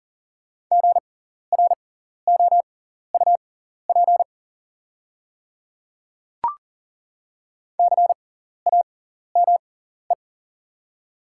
Morse code, heard as a keyed tone, about 41 wpm, spaced as for 14 wpm, 690 Hz. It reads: GROUP CAME
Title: 6x Character Spacing - Top 300 Words - 40wpm